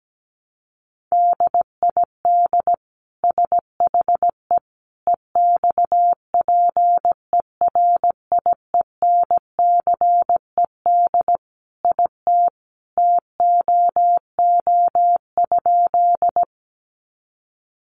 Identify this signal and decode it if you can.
Morse code, heard as a keyed tone, about 17 wpm, 700 Hz. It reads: DID SHE EXPERIENCED IT TOO?
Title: Sentences from Top 1100 Words 17wpm